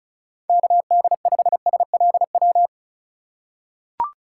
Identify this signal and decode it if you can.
Morse code, heard as a keyed tone, about 34 wpm, 700 Hz. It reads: KD5SLW
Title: USA Call Signs - 2x Character Spacing - 35wpm